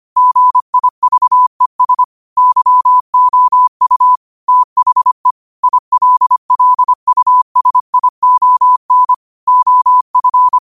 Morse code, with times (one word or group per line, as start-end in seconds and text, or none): 0.2-2.0 GIVES
2.4-4.2 YOU
4.5-5.3 THE
5.6-9.1 ILLUSION
9.5-10.6 OF